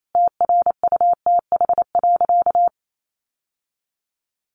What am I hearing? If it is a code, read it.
Morse code, heard as a keyed tone, about 28 wpm, 700 Hz. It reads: TRUTH.
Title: Sentences from Top 1100 Words 28wpm